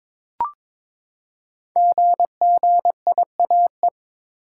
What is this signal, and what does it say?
Morse code, heard as a keyed tone, about 22 wpm, 690 Hz. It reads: GGIAE
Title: Code Groups - 5 Letters - 22wpm